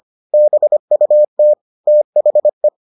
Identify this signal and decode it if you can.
Morse code, heard as a keyed tone, about 25 wpm, 600 Hz. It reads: BUT THE